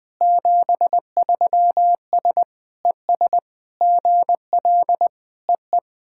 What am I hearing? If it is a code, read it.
Morse code, heard as a keyed tone, about 21 wpm, 700 Hz. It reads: 73S ES GL EE
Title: SOTA QSO Practice 20wpm